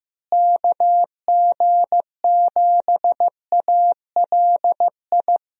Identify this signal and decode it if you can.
Morse code, heard as a keyed tone, about 15 wpm, 700 Hz. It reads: KG7ALI
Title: USA Call Signs - 6 Characters - Repeated 2x - 15wpm